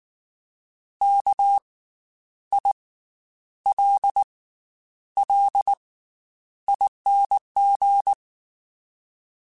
Morse code, heard as a keyed tone, about 19 wpm, spaced as for 9 wpm, 800 Hz. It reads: KILLING
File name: Special INGs / Special INGs-mp3s / Slow ing @18